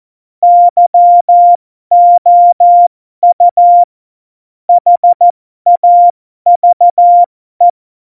Morse code, heard as a keyed tone, about 14 wpm, spaced as for 12 wpm, 700 Hz. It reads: YOU HAVE
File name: Morsekurs / Witze